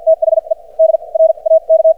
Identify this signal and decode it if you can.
Morse code, heard as a keyed tone, about 54 wpm, spaced as for 39 wpm, 590 Hz. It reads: THE DRAK
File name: archived / CRSnet